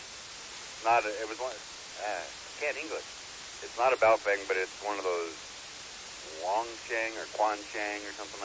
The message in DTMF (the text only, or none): none